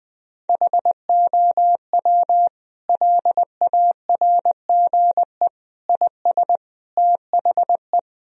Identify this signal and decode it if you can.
Morse code, heard as a keyed tone, about 20 wpm, 700 Hz. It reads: HOW LARGE IS THE
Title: Sentences from Top 1000 Words - Review Part 4 - 20wpm